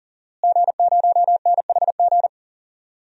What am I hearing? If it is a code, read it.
Morse code, heard as a keyed tone, about 40 wpm, 700 Hz. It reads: G0NHG
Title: DXCC-100 Call Signs - 5 Characters - Repeated 2x - 40wpm